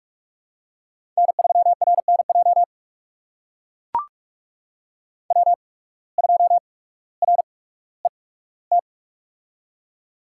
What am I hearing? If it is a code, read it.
Morse code, heard as a keyed tone, about 44 wpm, spaced as for 19 wpm, 690 Hz. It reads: N3RNJ W2RET
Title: USA Call Signs - 5 Characters - 8x Character Spacing - 45wpm